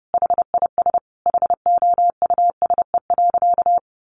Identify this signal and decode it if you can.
Morse code, heard as a keyed tone, about 30 wpm, 700 Hz. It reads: HIS HOUSE.
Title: Additional Easy Phrases 30wpm